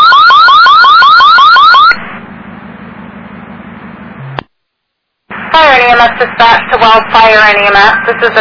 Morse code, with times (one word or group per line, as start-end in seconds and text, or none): none